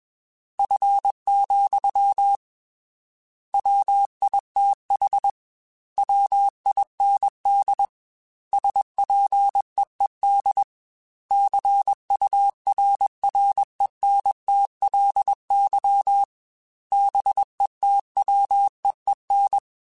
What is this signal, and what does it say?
Morse code, heard as a keyed tone, about 21 wpm, 800 Hz. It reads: F, WITH WIND SPEED CURRENTLY BETWEEN
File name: Weather Sets / Weather Sets-mp3s / Weather 5 5 @20